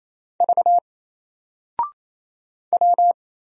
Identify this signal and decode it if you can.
Morse code, heard as a keyed tone, about 27 wpm, 700 Hz. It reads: V W
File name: Single-Letter-Number-28wpm